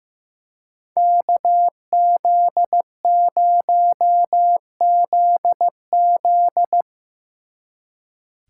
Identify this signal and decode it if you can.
Morse code, heard as a keyed tone, about 16 wpm, 700 Hz. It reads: KZ0ZZ
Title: Speed Racing - USA 2x2 Call Signs 15wpm